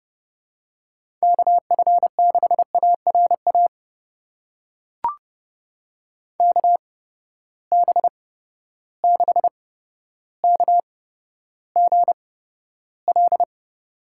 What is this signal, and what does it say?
Morse code, heard as a keyed tone, about 30 wpm, spaced as for 13 wpm, 700 Hz. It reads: KF6ARA KB6KGL
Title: USA Call Signs - 6 Characters - 8x Character Spacing - 30wpm